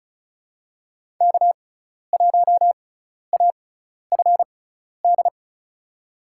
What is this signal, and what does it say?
Morse code, heard as a keyed tone, about 36 wpm, spaced as for 12 wpm, 700 Hz. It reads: K1AFD
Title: USA Call Signs - 5 Characters - 6x Character Spacing - 35wpm